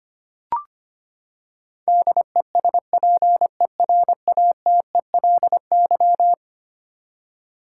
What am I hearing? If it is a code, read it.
Morse code, heard as a keyed tone, about 25 wpm, 700 Hz. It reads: DESPERATELY
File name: practice / 25wpm / Eleven Letter Words - 25wpm